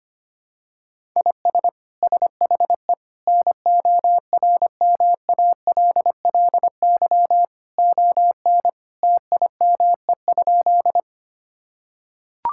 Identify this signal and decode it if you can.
Morse code, heard as a keyed tone, about 25 wpm, 700 Hz. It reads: IS SHE NORMALLY ON TIME?
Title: Sentences from Top 1000 Words - Review Part 6 - 25wpm